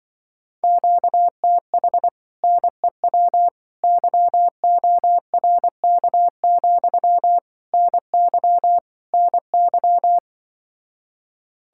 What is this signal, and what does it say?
Morse code, heard as a keyed tone, about 25 wpm, 700 Hz. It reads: QTH NEW YORK, NY NY